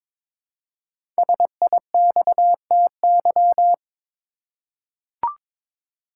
Morse code, heard as a keyed tone, about 22 wpm, 700 Hz. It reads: SIXTY
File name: 5 Letter Words - First 3 Letters Given - 22wpm